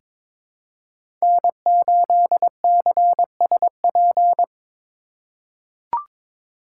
Morse code, heard as a keyed tone, about 22 wpm, 700 Hz. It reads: N8CSP
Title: USA Call Signs - 5 Characters - 3x Character Spacing - 22wpm